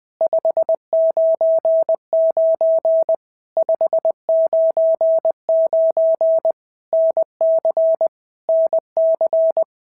Morse code, heard as a keyed tone, about 20 wpm, 650 Hz. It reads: 599 599 NC NC
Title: POTA QSO Practice - 20wpm